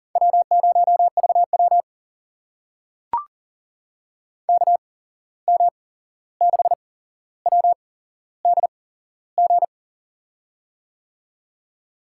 Morse code, heard as a keyed tone, about 38 wpm, spaced as for 15 wpm, 700 Hz. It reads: W0VW KM6WDG
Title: USA Call Signs - 8x Character Spacing - 40wpm